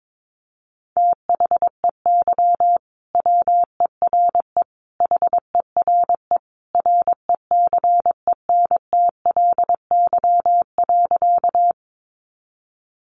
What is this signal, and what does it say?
Morse code, heard as a keyed tone, about 22 wpm, 700 Hz. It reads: THEY WERE HERE RECENTLY.
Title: Sentences from Top 1200 Words 22wpm